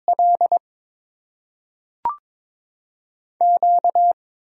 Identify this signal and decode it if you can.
Morse code, heard as a keyed tone, about 22 wpm, 700 Hz. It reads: L Q
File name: Top 90 QSO Elements - 8x Character Spacing - 22wpm